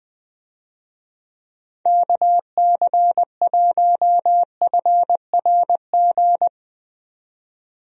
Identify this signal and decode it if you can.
Morse code, heard as a keyed tone, about 21 wpm, 700 Hz. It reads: KC1FRG
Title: USA Call Signs - 6 Characters - 20wpm